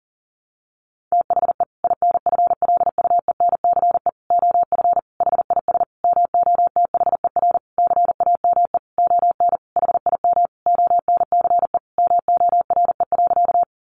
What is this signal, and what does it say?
Morse code, heard as a keyed tone, about 40 wpm, 700 Hz. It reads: THE INFLUENCE OF HIS MOTHER CAME ON HIM ONCE MORE.
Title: Sentences from Top 1100 Words 40wpm